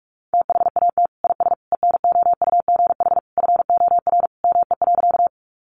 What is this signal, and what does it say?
Morse code, heard as a keyed tone, about 44 wpm, 700 Hz. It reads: THAT IS ENOUGH FOR ME.
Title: Sentences from Top 400 Words 45wpm